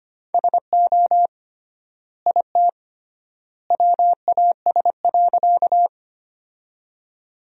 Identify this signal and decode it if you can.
Morse code, heard as a keyed tone, about 25 wpm, 700 Hz. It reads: SO IT WAS.
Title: 3x Word-Spacing - Sentences from Top 100 Words - 25wpm